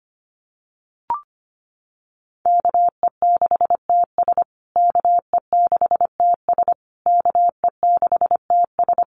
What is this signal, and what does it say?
Morse code, heard as a keyed tone, about 25 wpm, 700 Hz. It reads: KE6TS KE6TS KE6TS
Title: USA Call Signs Repeated 3x 25wpm